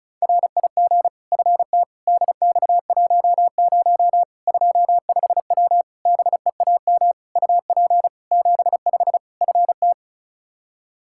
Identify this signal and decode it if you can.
Morse code, heard as a keyed tone, about 36 wpm, 700 Hz. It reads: RIG FT DX10 25W BEAM UP 75 FT
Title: Practice QSO - Exchange 2 - Info - 35wpm